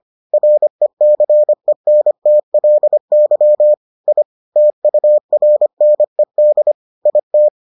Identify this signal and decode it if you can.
Morse code, heard as a keyed tone, about 25 wpm, 600 Hz. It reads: RECENTLY I TURNED IT